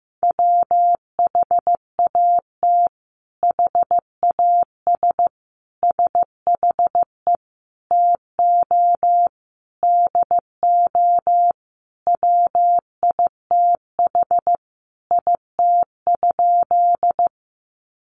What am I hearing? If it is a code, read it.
Morse code, heard as a keyed tone, about 15 wpm, 700 Hz. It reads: WHAT HAS SHE TO DO WITH IT?
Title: Sentences from Top 100 Words 15wpm